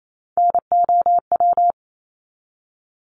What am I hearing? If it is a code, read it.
Morse code, heard as a keyed tone, about 28 wpm, 700 Hz. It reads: NOW
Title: Top 200 Words 28wpm